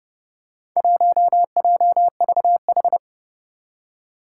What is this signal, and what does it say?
Morse code, heard as a keyed tone, about 30 wpm, 700 Hz. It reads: 1JVH
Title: Four Challenging Letter-Number Combinations 30wpm